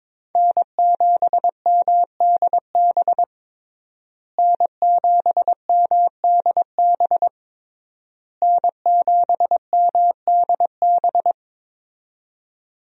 Morse code, heard as a keyed tone, about 22 wpm, 700 Hz. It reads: N7MDB N7MDB N7MDB
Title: USA Call Signs - 5 Characters - Repeated 3x - 22wpm